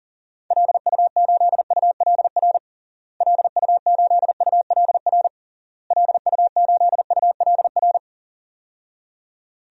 Morse code, heard as a keyed tone, about 40 wpm, 700 Hz. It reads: LU8ULR LU8ULR LU8ULR
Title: DXCC-100 Call Signs - 6 Characters - Repeated 3x - 40wpm